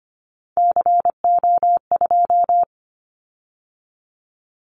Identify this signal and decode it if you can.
Morse code, heard as a keyed tone, about 25 wpm, 700 Hz. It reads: CO2
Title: Callsign Prefixes for Top 100 DXCC - 25wpm